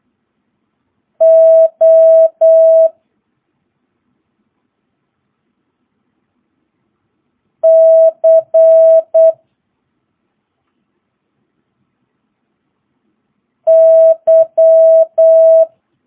Morse code, heard as a keyed tone, about 8 wpm, 650 Hz. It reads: O C Y
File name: jh8cbh / 2022cw